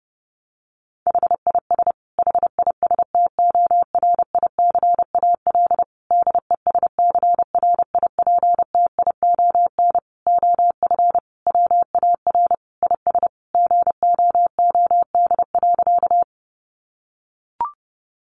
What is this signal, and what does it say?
Morse code, heard as a keyed tone, about 30 wpm, 700 Hz. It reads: HIS HISTORICAL DESCRIPTION OF WAR IS GOOD.
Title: Sentences from Top 1300 Words 30wpm